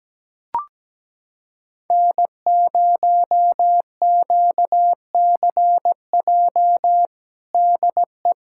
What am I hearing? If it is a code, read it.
Morse code, heard as a keyed tone, about 17 wpm, 700 Hz. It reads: N0QCJ DE
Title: Practice QSO - Exchange 4 - Ending - 17wpm